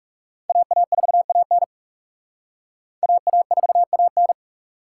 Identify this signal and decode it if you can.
Morse code, heard as a keyed tone, about 38 wpm, 700 Hz. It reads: AA4AN AA4AN